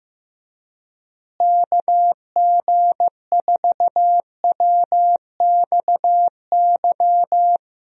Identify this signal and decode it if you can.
Morse code, heard as a keyed tone, about 15 wpm, 700 Hz. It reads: KG4WXY
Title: USA Call Signs - 6 Characters - Repeated 3x - 15wpm